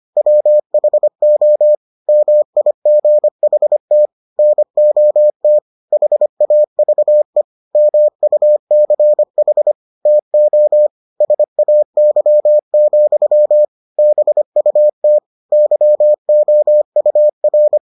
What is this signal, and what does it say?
Morse code, heard as a keyed tone, about 25 wpm, 600 Hz. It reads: WHO MIGHT NOT HAVE MUCH TO SAY, BUT YOUR